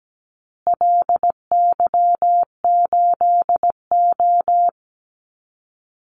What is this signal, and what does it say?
Morse code, heard as a keyed tone, about 17 wpm, 700 Hz. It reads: LY8O
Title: Call Signs Repeated 2x - 17wpm